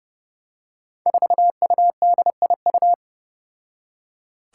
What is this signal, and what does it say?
Morse code, heard as a keyed tone, about 30 wpm, 700 Hz. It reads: 4UDIU
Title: Five Challenging Letter-Number Combinations 30wpm